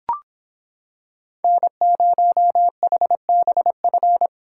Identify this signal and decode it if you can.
Morse code, heard as a keyed tone, about 25 wpm, 700 Hz. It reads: N0HBF